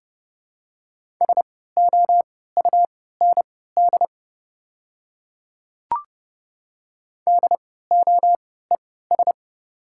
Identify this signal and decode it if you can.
Morse code, heard as a keyed tone, about 30 wpm, spaced as for 17 wpm, 700 Hz. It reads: SOUND DOES
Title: 3x Character Spacing - Top 200 Words - 30wpm